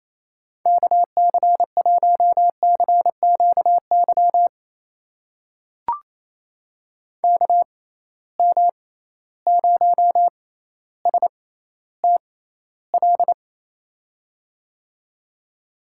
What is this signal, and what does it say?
Morse code, heard as a keyed tone, about 28 wpm, spaced as for 14 wpm, 700 Hz. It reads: KC1CQY KM0STL
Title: USA Call Signs - 6 Characters - 6x Character Spacing - 28wpm